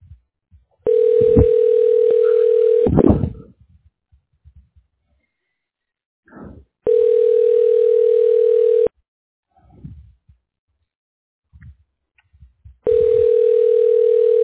Ringback tone (North America), 440 + 480 Hz.